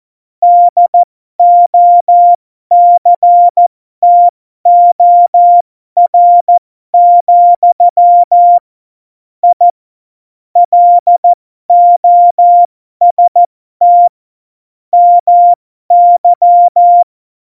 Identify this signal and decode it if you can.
Morse code, heard as a keyed tone, about 14 wpm, spaced as for 12 wpm, 700 Hz. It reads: DOCTOR, I LOST MY
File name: Morsekurs / Witze